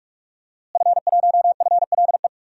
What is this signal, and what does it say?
Morse code, heard as a keyed tone, about 46 wpm, 700 Hz. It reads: F1FLE
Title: DXCC-100 Call Signs - 5 Characters - Repeated 2x - 45wpm